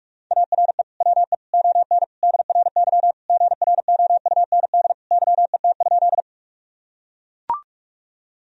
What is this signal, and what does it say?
Morse code, heard as a keyed tone, about 43 wpm, 700 Hz. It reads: ARE WE ON DRY GROUND YET?
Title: Sentences from Top 1000 Words - Review Part 3 - 45wpm